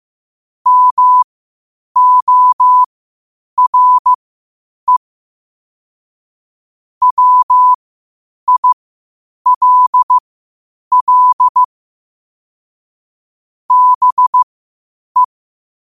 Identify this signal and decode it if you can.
Morse code, heard as a keyed tone, about 15 wpm, spaced as for 8 wpm, 1000 Hz. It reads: MORE WILL BE
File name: 8wpm